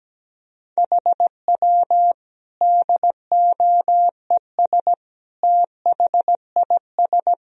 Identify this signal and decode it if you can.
Morse code, heard as a keyed tone, about 17 wpm, 700 Hz. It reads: HW DOES THIS
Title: Sentences from Top 400 Words and Abbreviations 17wpm